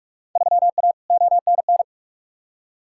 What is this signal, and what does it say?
Morse code, heard as a keyed tone, about 47 wpm, 700 Hz. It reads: 3A ONN
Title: Field Day Exchanges Encore - 45wpm